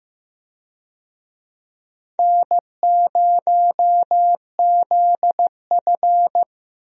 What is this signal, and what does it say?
Morse code, heard as a keyed tone, about 15 wpm, 700 Hz. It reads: N0ZF
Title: USA Call Signs - 4 Characters - Repeated 3x - 15wpm